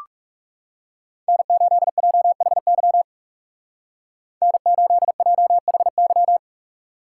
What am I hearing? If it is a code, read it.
Morse code, heard as a keyed tone, about 36 wpm, 700 Hz. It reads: N8JHY N8JHY
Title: Speed Racing - USA 1x3 Call Signs 30wpm